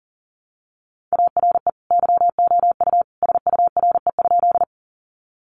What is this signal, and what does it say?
Morse code, heard as a keyed tone, about 40 wpm, 700 Hz. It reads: ARE YOU SURE?
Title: Sentences from Top 300 Words 40wpm